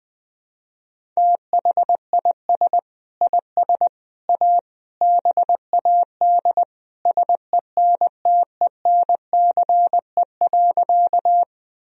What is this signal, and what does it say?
Morse code, heard as a keyed tone, about 20 wpm, 700 Hz. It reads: THIS IS A BAD SENTENCE.
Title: Sentences from Top 1000 Words - Review Part 1 - 20wpm